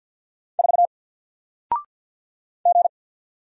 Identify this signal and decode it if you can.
Morse code, heard as a keyed tone, about 50 wpm, 700 Hz. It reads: 4 G
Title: Single Character (Letter:Number:Punctuation:Prosign) 50wpm